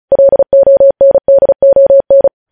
Morse code, heard as a keyed tone, about 36 wpm, 550 Hz. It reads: LONDON
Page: news headlines in morse code at 35 wpm